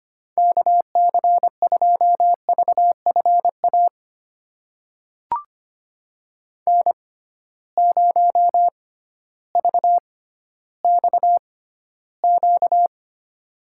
Morse code, heard as a keyed tone, about 25 wpm, spaced as for 14 wpm, 700 Hz. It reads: KC2VFA N0VXQ
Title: USA Call Signs - 6x Character Spacing - 25wpm